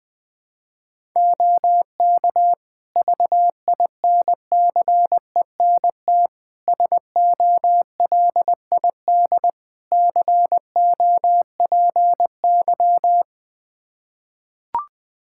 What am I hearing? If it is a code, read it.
Morse code, heard as a keyed tone, about 20 wpm, 700 Hz. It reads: OK VINCENT SOLID COPY
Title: Practice QSO - Exchange 3 - More Info - 20wpm